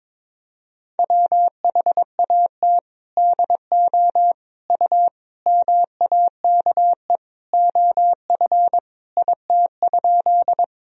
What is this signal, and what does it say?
Morse code, heard as a keyed tone, about 22 wpm, 700 Hz. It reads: WHAT DO U MAKE OF IT?
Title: Sentences from Top 100 Words and Abbreviations 22wpm